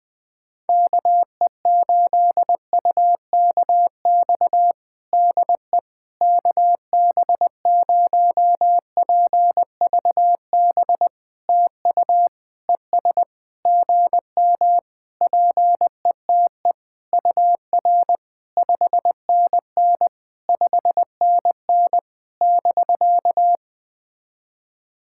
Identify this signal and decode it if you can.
Morse code, heard as a keyed tone, about 20 wpm, 700 Hz. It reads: KE8UKX DE KB0PVB TU ES GM PETE UR 5NN 5NN <BK>
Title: POTA QSO Practice - 20wpm